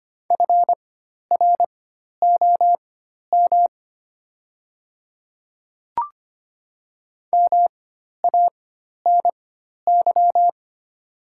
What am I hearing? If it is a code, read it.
Morse code, heard as a keyed tone, about 25 wpm, spaced as for 12 wpm, 700 Hz. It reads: FROM MANY
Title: 4x Character Spacing - Top 100 Words - 25wpm